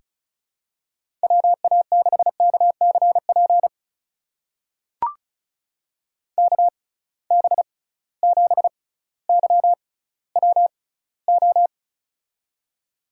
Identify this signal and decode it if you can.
Morse code, heard as a keyed tone, about 36 wpm, spaced as for 18 wpm, 700 Hz. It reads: WA6KCP KB7YWO
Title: USA Call Signs - 6 Characters - 6x Character Spacing - 35wpm